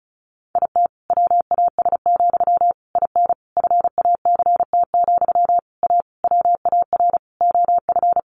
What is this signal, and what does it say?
Morse code, heard as a keyed tone, about 34 wpm, 700 Hz. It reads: IT WAS, IN FACT, A WAR OF
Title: Sentences from Top 1400 Words 35wpm